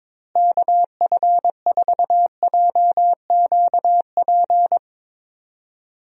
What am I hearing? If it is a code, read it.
Morse code, heard as a keyed tone, about 23 wpm, 700 Hz. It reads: KF4JQP
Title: USA Call Signs - 6 Characters - Repeated 2x - 22wpm